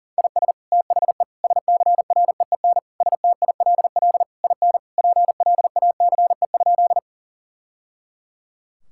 Morse code, heard as a keyed tone, about 37 wpm, 700 Hz. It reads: IS THE SCREEN STILL IN PLACE?
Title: Sentences from Top 1000 Words - Review Part 6 - 40wpm